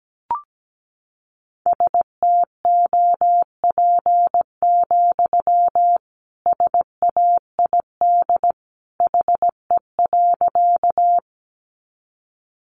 Morse code, heard as a keyed tone, about 17 wpm, 700 Hz. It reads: STOP, SAID HE.
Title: Sentences from Top 300 Words 17wpm